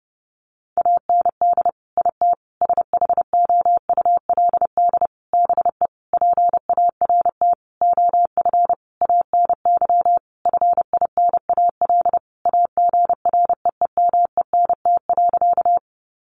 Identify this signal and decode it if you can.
Morse code, heard as a keyed tone, about 28 wpm, 700 Hz. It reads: AND IT SHOULD BE PART OF ANY FINAL AGREEMENT.
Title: Sentences from Top 1500 Words 30wpm